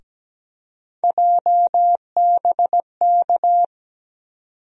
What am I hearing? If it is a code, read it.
Morse code, heard as a keyed tone, about 17 wpm, 700 Hz. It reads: JBK
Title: USA Call Sign Postfixes - 3 Letters 17wpm